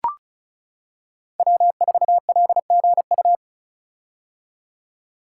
Morse code, heard as a keyed tone, about 36 wpm, 700 Hz. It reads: W4LGU
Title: Code Groups - 5 Letter:Numbers - 35wpm